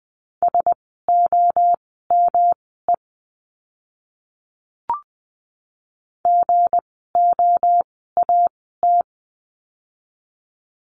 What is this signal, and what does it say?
Morse code, heard as a keyed tone, about 19 wpm, spaced as for 14 wpm, 700 Hz. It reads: SOME GOAT
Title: 160 Lesson 26f - Letter g - Words 20wpm